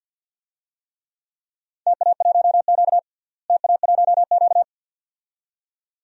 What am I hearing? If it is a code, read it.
Morse code, heard as a keyed tone, about 53 wpm, 700 Hz. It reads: TA1Q TA1Q
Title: DXCC-100 Call Signs - 4 Characters - Repeated 2x - 50wpm